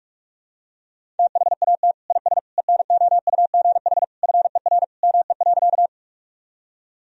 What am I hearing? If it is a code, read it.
Morse code, heard as a keyed tone, about 43 wpm, 700 Hz. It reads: THAT IS ENOUGH FER ME.
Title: Sentences from Top 400 Words and Abbreviations 45wpm